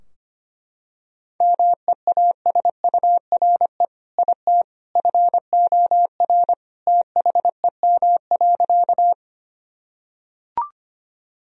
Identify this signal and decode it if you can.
Morse code, heard as a keyed tone, about 25 wpm, 700 Hz. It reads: MEASURE IT FOR THEM.
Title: Sentences from Top 1000 Words - Review Part 2 - 25wpm